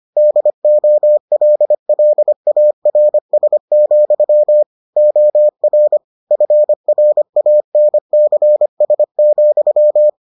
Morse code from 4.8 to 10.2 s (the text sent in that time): OR FRANCS,